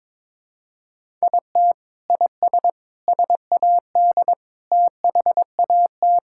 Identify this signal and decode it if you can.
Morse code, heard as a keyed tone, about 22 wpm, 700 Hz. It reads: IT IS SAD THAT
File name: practice / 22wpm / Sentences with 4-Word Phrases - 22wpm